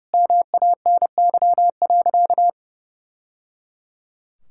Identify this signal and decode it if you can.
Morse code, heard as a keyed tone, about 30 wpm, 700 Hz. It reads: MANY.
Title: Sentences from Top 1000 Words - Review Part 6 - 30wpm